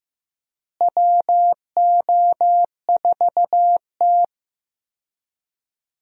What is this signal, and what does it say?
Morse code, heard as a keyed tone, about 15 wpm, 700 Hz. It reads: WO4T
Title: USA Call Signs - 4 Characters - Repeated 3x - 15wpm